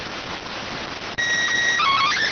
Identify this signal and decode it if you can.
SSTV (slow-scan television), PD 290.